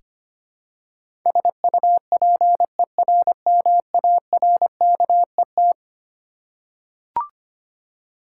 Morse code, heard as a keyed tone, about 25 wpm, 700 Hz. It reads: SUPERMARKET
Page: morse-code-ninja-practice-25wpm